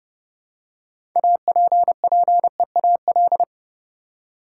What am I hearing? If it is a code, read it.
Morse code, heard as a keyed tone, about 31 wpm, 700 Hz. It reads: APPEAL
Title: Six Letter Words - 30wpm